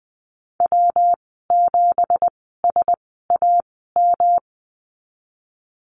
Morse code, heard as a keyed tone, about 20 wpm, spaced as for 15 wpm, 700 Hz. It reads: W7SAM
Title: 179 Lesson 29g - Slash - Call Signs 20wpm